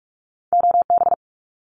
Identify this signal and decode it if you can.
Morse code, heard as a keyed tone, about 46 wpm, 700 Hz. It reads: OB
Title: Abbreviations Repeated 3x - 45wpm